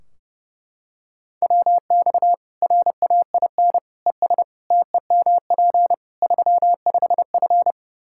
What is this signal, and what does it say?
Morse code, heard as a keyed tone, about 30 wpm, 700 Hz. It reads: WX RAIN ES TEMP 35F
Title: Practice QSO - Exchange 2 - Info - 30wpm